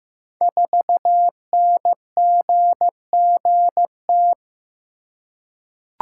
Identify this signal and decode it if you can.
Morse code, heard as a keyed tone, about 15 wpm, 700 Hz. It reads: 4NGGT
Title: Code Groups - 5 Letter:Numbers - 15wpm